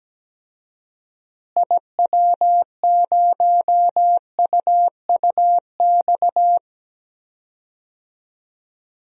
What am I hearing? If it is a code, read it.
Morse code, heard as a keyed tone, about 17 wpm, 700 Hz. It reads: IW0UUX
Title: DXCC-100 Call Signs - 6 Characters - Repeated 2x - 17wpm